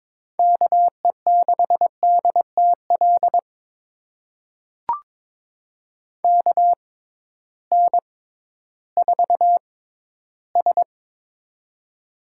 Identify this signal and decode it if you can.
Morse code, heard as a keyed tone, about 22 wpm, spaced as for 13 wpm, 700 Hz. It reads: KE6DTL KN4S
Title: USA Call Signs - 6x Character Spacing - 22wpm